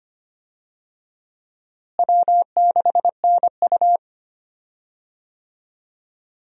Morse code, heard as a keyed tone, about 25 wpm, 700 Hz. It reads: W6NU
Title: USA Call Signs - 4 Characters - Repeated 3x - 25wpm